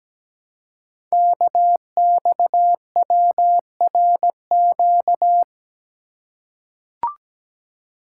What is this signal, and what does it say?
Morse code, heard as a keyed tone, about 17 wpm, 700 Hz. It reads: KXWRQ
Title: Code Groups - 5 Letters - 17wpm